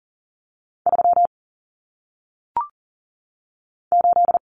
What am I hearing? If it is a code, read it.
Morse code, heard as a keyed tone, about 40 wpm, 700 Hz. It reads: 3 8
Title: Common Cut Numbers - 1-Digit 40wpm